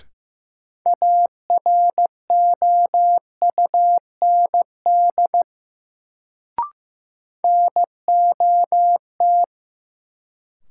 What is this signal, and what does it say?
Morse code, heard as a keyed tone, about 15 wpm, 700 Hz. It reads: AROUND NOT
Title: Sentences from Top 600 Words - New Word Drill - 15wpm